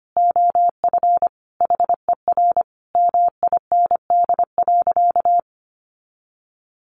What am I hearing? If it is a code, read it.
Morse code, heard as a keyed tone, about 25 wpm, 700 Hz. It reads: OF HER MIND.
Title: Sentences from Top 1300 Words 25wpm